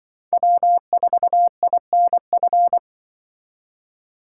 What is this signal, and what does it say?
Morse code, heard as a keyed tone, about 24 wpm, 700 Hz. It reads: W4INF
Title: Speed Racing - USA 1x3 Call Signs 20wpm